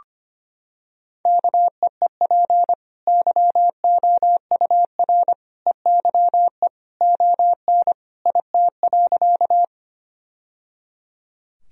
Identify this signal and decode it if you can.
Morse code, heard as a keyed tone, about 25 wpm, 700 Hz. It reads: KEEP YOUR EYE ON IT.
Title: Sentences from Top 1000 Words - Review Part 1 - 25wpm